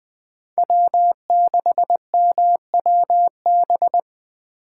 Morse code, heard as a keyed tone, about 20 wpm, 700 Hz. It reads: W6MWB
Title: USA Call Signs - 5 Characters - 3x Character Spacing - 20wpm